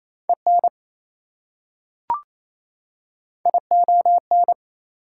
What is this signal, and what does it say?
Morse code, heard as a keyed tone, about 28 wpm, 700 Hz. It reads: EN ION
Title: Prefixes and Suffixes 28wpm